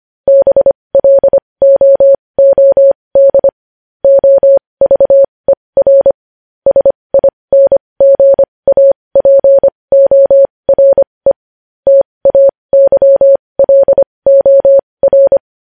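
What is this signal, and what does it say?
Morse code, heard as a keyed tone, about 25 wpm, spaced as for 20 wpm, 550 Hz. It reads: BLOOD OVER SINGAPORE TAYLOR
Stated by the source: News headlines sent in Morse code at 20.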